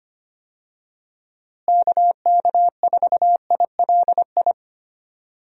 Morse code, heard as a keyed tone, about 25 wpm, 700 Hz. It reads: KK4ILI
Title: USA Call Signs - 6 Characters - Repeated 3x - 25wpm